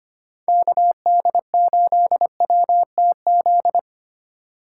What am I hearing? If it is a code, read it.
Morse code, heard as a keyed tone, about 25 wpm, 700 Hz. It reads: KD8WTZ
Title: USA Call Signs - 6 Characters - 4x Character Spacing - 25wpm